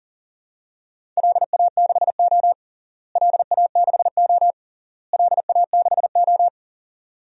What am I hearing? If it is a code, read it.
Morse code, heard as a keyed tone, about 40 wpm, 700 Hz. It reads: LA6O LA6O LA6O
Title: DXCC-100 Call Signs - 4 Characters - Repeated 3x - 40wpm